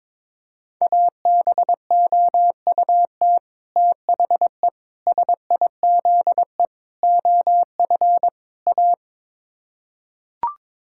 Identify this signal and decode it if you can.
Morse code, heard as a keyed tone, about 21 wpm, 700 Hz. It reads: ABOUT THE SIZE OF A
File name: practice / 22wpm / Build A Phrase - 5-Word Phrases - 22wpm